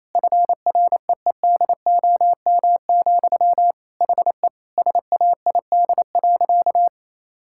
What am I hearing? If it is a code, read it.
Morse code, heard as a keyed tone, about 28 wpm, 700 Hz. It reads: FREEDOM, HE SAID.
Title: Sentences from Top 1000 Words - Review Part 5 - 28wpm